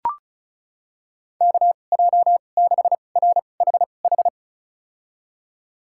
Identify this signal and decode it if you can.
Morse code, heard as a keyed tone, about 34 wpm, spaced as for 25 wpm, 700 Hz. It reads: KJ6RHH
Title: USA Call Signs - 6 Characters - 2x Character Spacing - 35wpm